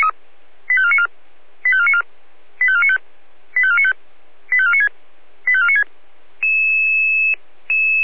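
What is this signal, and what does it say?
Five-tone selective call (ZVEI), heard as quick runs of five tones, tones of about 70 ms. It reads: 85494 85494 85495 85495 85496 85496
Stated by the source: Five-tone ZVEI selective call received on an SDR, channel 496.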